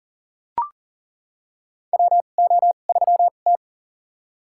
Morse code, heard as a keyed tone, about 39 wpm, spaced as for 29 wpm, 700 Hz. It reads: WO3T
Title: USA Call Signs - 4 Characters - 2x Character Spacing - 40wpm